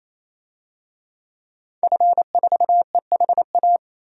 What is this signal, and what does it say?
Morse code, heard as a keyed tone, about 27 wpm, 700 Hz. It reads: F4EHA
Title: DXCC-100 Call Signs - 5 Characters - 28wpm